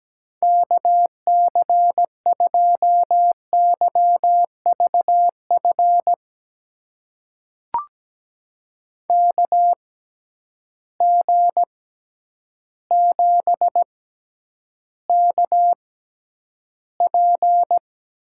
Morse code, heard as a keyed tone, about 17 wpm, spaced as for 9 wpm, 700 Hz. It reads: KC2YVF KG7KP
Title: USA Call Signs - 6 Characters - 6x Character Spacing - 17wpm